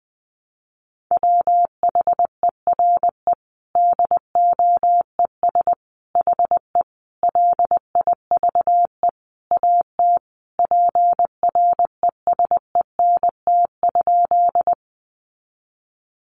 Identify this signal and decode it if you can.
Morse code, heard as a keyed tone, about 20 wpm, 700 Hz. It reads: WHERE DOES HE LIVE AT PRESENT?
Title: Sentences from Top 800 Words 20wpm